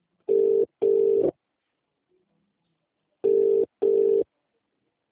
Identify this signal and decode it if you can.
Ringback tone (UK), 400 + 450 Hz.